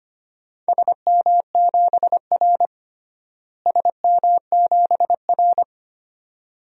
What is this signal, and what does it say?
Morse code, heard as a keyed tone, about 25 wpm, 700 Hz. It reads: SM7R SM7R
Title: DXCC-100 Call Signs - 4 Characters - Repeated 2x - 25wpm